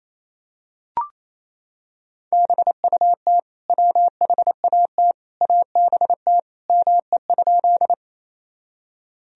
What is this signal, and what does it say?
Morse code, heard as a keyed tone, about 28 wpm, 700 Hz. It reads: BUT WHAT ABT ME?
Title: Sentences from Top 200 Words and Abbreviations 28wpm